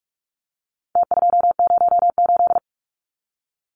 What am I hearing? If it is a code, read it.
Morse code, heard as a keyed tone, about 46 wpm, 700 Hz. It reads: T208
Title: Numbers - Contest Style - 4-Digit 45wpm